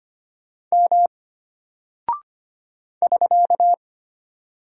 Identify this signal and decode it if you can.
Morse code, heard as a keyed tone, about 25 wpm, 700 Hz. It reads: M <SK>
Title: Single Character (Letter:Number:Punctuation:Prosign) Alt 25wpm